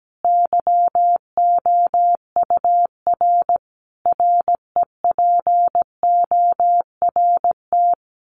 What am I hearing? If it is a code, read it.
Morse code, heard as a keyed tone, about 17 wpm, 700 Hz. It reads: YOUR REPORT
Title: Common QSO Phrases 17wpm